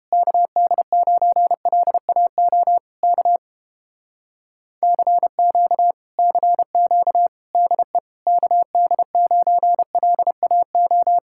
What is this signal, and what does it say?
Morse code, heard as a keyed tone, about 31 wpm, 700 Hz. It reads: KD9LAO K CQ CQ DE KD9LAO